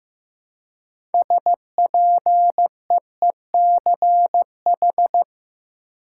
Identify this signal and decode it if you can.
Morse code, heard as a keyed tone, about 15 wpm, 700 Hz. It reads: SPEECH
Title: Six Letter Words - 15wpm